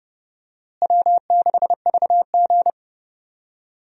Morse code, heard as a keyed tone, about 30 wpm, 700 Hz. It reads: W6VG
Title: USA Call Signs - 4 Characters - Repeated 2x - 30wpm